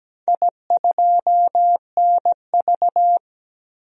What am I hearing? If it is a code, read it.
Morse code, heard as a keyed tone, about 17 wpm, 700 Hz. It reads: I2NV
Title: DXCC-100 Call Signs - 4 Characters - 17wpm